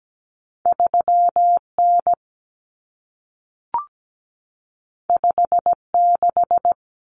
Morse code, heard as a keyed tone, about 17 wpm, 700 Hz. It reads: 3N 56
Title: Common Cut Numbers - 2-Digit 17wpm